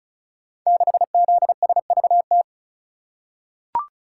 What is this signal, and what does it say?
Morse code, heard as a keyed tone, about 36 wpm, 700 Hz. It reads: 6ZSVT
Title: Code Groups - 5 Letter:Numbers - 35wpm